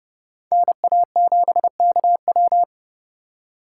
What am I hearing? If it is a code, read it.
Morse code, heard as a keyed tone, about 39 wpm, 700 Hz. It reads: NA7KW